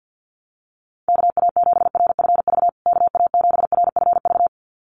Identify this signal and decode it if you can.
Morse code, heard as a keyed tone, about 50 wpm, 700 Hz. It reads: KA7RFV KA7RFV
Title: USA Call Signs Repeated 2x 50wpm